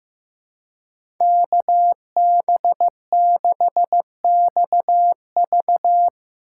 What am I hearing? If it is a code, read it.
Morse code, heard as a keyed tone, about 15 wpm, 700 Hz. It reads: KB6XV
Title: USA Call Signs - 5 Characters - 2x Character Spacing - 15wpm